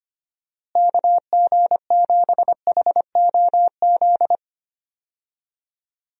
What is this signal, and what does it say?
Morse code, heard as a keyed tone, about 25 wpm, 700 Hz. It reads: KG7HOZ
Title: USA Call Signs - 6 Characters - Repeated 3x - 25wpm